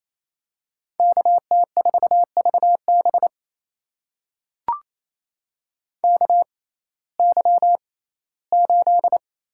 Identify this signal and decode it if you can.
Morse code, heard as a keyed tone, about 28 wpm, spaced as for 17 wpm, 700 Hz. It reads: KT4VB KY8
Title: USA Call Signs - 6x Character Spacing - 28wpm